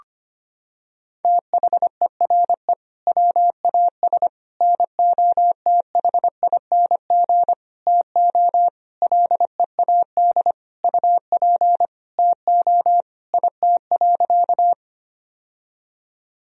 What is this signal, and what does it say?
Morse code, heard as a keyed tone, about 25 wpm, 700 Hz. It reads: THERE WAS NOTHING TO LEAD UP TO IT.
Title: Sentences from Top 1000 Words - Review Part 2 - 25wpm